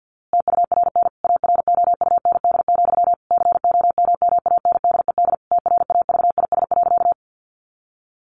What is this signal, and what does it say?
Morse code, heard as a keyed tone, about 50 wpm, 700 Hz. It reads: TURN AROUND, COMMANDED TRAVIS.
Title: A Sound Of Thunder 50wpm